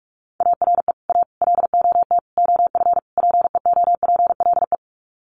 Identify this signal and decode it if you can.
Morse code, heard as a keyed tone, about 44 wpm, 700 Hz. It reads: ARE A LOT OF PEOPLE
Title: Sets of 5 Words 45wpm